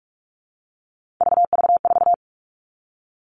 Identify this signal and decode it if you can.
Morse code, heard as a keyed tone, about 44 wpm, 700 Hz. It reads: VV4
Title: All Cut Numbers - 3-Digit 45wpm